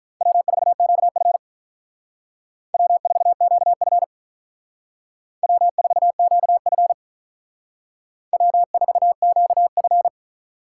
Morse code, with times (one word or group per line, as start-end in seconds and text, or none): 0.2-1.4 W4QF
2.7-4.0 W4QF
5.4-6.9 W4QF
8.3-10.1 W4QF